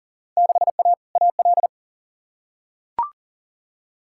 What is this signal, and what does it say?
Morse code, heard as a keyed tone, about 40 wpm, 700 Hz. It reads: 6A AL
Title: Field Day Exchanges Encore - 40wpm